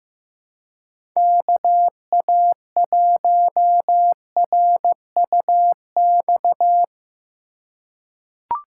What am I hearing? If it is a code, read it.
Morse code, heard as a keyed tone, about 16 wpm, 700 Hz. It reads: KA1RUX
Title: USA Call Signs - 6 Characters - 3x Character Spacing - 15wpm